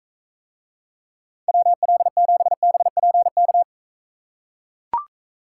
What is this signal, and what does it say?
Morse code, heard as a keyed tone, about 42 wpm, 700 Hz. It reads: WL7BPK